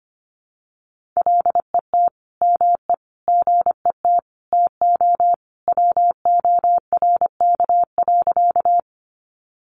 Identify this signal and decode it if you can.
Morse code, heard as a keyed tone, about 25 wpm, 700 Hz. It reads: LET ME GET TO WORK.
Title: Sentences from Top 300 Words 25wpm